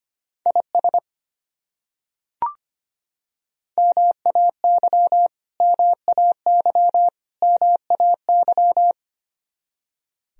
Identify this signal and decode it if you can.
Morse code, heard as a keyed tone, about 25 wpm, 700 Hz. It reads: IS MAY MAY MAY
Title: Top 100 Words Repeated 25wpm